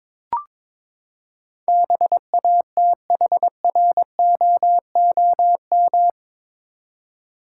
Morse code, heard as a keyed tone, about 22 wpm, 700 Hz. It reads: BATHROOM
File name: practice / 22wpm / Eight Letter Words - 22wpm